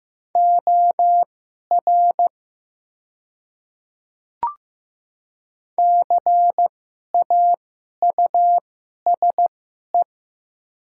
Morse code, heard as a keyed tone, about 15 wpm, spaced as for 11 wpm, 690 Hz. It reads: OR CAUSE
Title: 2x Character Spacing - Top 200 Words - 15wpm